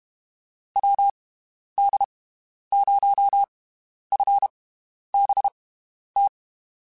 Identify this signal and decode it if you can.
Morse code, heard as a keyed tone, about 32 wpm, 800 Hz. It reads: W D 0 F B T
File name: (spaced) / QSOs Spaced-mp3s / WD0FBT 1 @30 sp